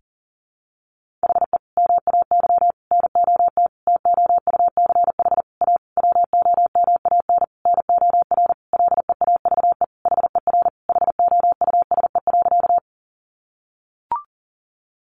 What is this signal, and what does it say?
Morse code, heard as a keyed tone, about 40 wpm, 700 Hz. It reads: HE MAY NOT TOUCH A WOMAN NOR LEAVE HER HOUSE.
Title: Sentences from Top 1500 Words 40wpm